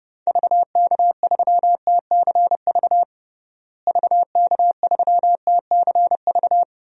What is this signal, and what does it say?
Morse code, heard as a keyed tone, about 31 wpm, 700 Hz. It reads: VK3TCV VK3TCV
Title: DXCC-100 Call Signs - 6 Characters - Repeated 2x - 30wpm